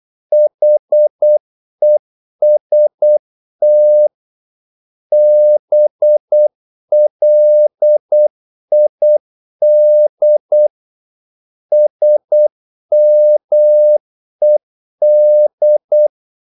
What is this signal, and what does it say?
Morse code, heard as a keyed tone, about 8 wpm, 600 Hz. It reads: HEST BLID SMED
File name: Korte ord DK 8wpm